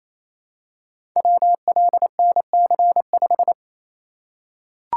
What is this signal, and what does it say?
Morse code, heard as a keyed tone, about 27 wpm, 700 Hz. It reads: WLNC5
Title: Code Groups - 5 Letter:Numbers - 28wpm